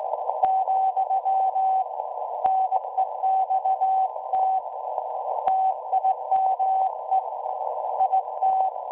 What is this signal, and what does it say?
Morse code, heard as a keyed tone, about 17 wpm, 760 Hz. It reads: , NEXT TIME IT
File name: WebSDR records / SAQ at PI4THT part od 1005z 17-2kHz